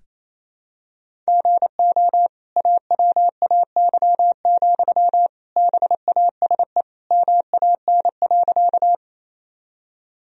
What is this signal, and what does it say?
Morse code, heard as a keyed tone, about 28 wpm, 700 Hz. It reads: GO AWAY, BASE MAN.
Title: Sentences from Top 1000 Words - Review Part 1 - 28wpm